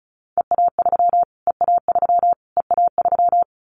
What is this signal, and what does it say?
Morse code, heard as a keyed tone, about 34 wpm, 700 Hz. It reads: EA3 EA3 EA3
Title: Callsign Prefixes for Top 25 DXCC - Repeated 3x - 35wpm